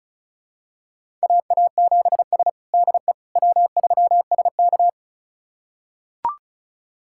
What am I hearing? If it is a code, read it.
Morse code, heard as a keyed tone, about 36 wpm, 700 Hz. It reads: AA7S DE W3SK
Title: USA Callsign DE Callsign - 4 Characters - 35wpm